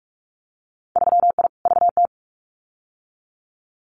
Morse code, heard as a keyed tone, about 44 wpm, 700 Hz. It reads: 3I VT
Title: Winter Field Day Exchanges Encore 45wpm